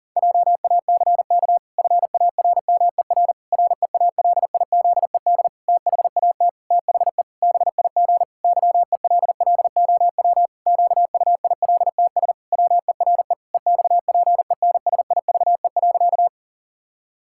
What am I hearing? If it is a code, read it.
Morse code, heard as a keyed tone, about 40 wpm, 700 Hz. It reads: JACK FARMER REALIZED THAT THE BIG YELLOW QUILTS WERE EXPENSIVE.
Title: Pangram Challenge v2 - 40wpm